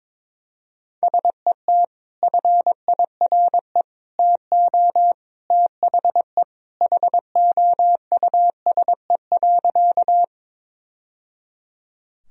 Morse code, heard as a keyed tone, about 22 wpm, 700 Hz. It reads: SET FIRE TO THE HOUSE.
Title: Sentences from Top 1000 Words - Review Part 1 - 22wpm